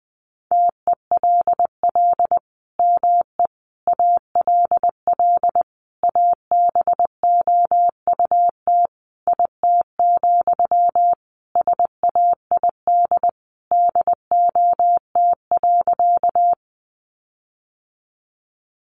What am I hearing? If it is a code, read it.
Morse code, heard as a keyed tone, about 20 wpm, 700 Hz. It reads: TELL ME ALL ABOUT IT, SAID DOT.
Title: Sentences from Top 1500 Words 20wpm